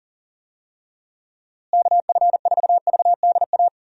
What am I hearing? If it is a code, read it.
Morse code, heard as a keyed tone, about 40 wpm, 700 Hz. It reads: KF4VDA
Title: USA Call Signs - 6 Characters - Repeated 2x - 40wpm